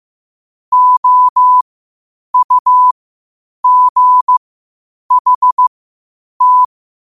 Morse code, heard as a keyed tone, about 15 wpm, spaced as for 8 wpm, 1000 Hz. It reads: OUGHT
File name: qotdfiles / 8wpm